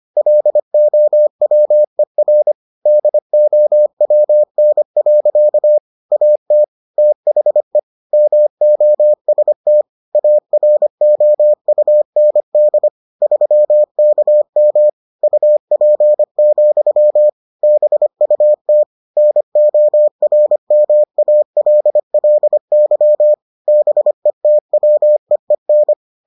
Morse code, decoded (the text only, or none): LOWER DOWN. AT THE MOST AROUND 3KM UP, BUT NORMALLY BETWEEN